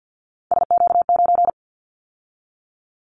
Morse code, heard as a keyed tone, about 50 wpm, 700 Hz. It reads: SQ9
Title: Callsign Prefixes for Top 25 DXCC - 50wpm